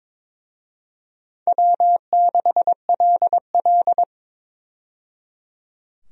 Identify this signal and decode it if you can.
Morse code, heard as a keyed tone, about 22 wpm, 700 Hz. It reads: W6LL
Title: USA Call Signs - 4 Characters - 22wpm